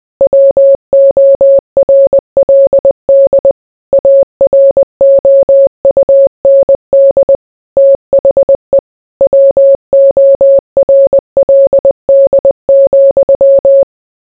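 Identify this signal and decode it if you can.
Morse code, heard as a keyed tone, about 20 wpm, 550 Hz. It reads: WORLD AROUND THE WORLD,
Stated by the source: Morse code mixed with noise and music as background sound for sleep.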